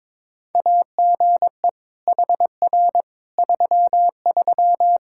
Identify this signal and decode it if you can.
Morse code, heard as a keyed tone, about 22 wpm, 700 Hz. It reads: AGE HR 33
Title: Practice QSO - Exchange 3 - More Info - 22wpm